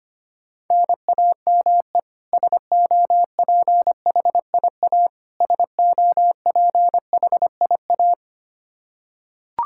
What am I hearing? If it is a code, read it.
Morse code, heard as a keyed tone, about 25 wpm, 700 Hz. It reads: NAME SOPHIA SOPHIA
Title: Practice QSO - Exchange 1 - Beginning - 25wpm